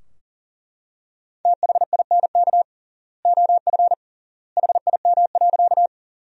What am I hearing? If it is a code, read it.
Morse code, heard as a keyed tone, about 37 wpm, 700 Hz. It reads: THINK OF HIM.
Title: 3x Word-Spacing - Sentences from Top 200 Words - 40wpm